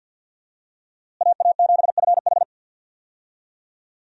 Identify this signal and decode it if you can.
Morse code, heard as a keyed tone, about 51 wpm, 700 Hz. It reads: AA7FH
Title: USA Call Signs - 5 Characters - Repeated 3x - 50wpm